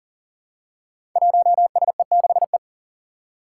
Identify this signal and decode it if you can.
Morse code, heard as a keyed tone, about 43 wpm, 700 Hz. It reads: 1SE6E
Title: Five Challenging Letter-Number Combinations 40wpm